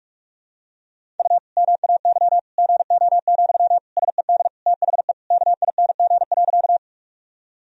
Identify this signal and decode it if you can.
Morse code, heard as a keyed tone, about 41 wpm, 700 Hz. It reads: U MAY GO, SED THE KING.
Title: Sentences from Top 400 Words and Abbreviations 45wpm